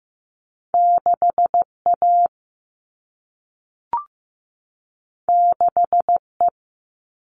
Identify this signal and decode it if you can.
Morse code, heard as a keyed tone, about 15 wpm, 700 Hz. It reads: 6A 6E
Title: All Cut Numbers - 2-Digit 15wpm